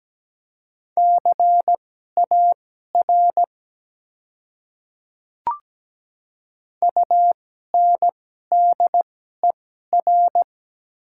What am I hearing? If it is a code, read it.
Morse code, heard as a keyed tone, about 17 wpm, spaced as for 12 wpm, 700 Hz. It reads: CAR UNDER
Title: 2x Character Spacing - Top 300 Words - 17wpm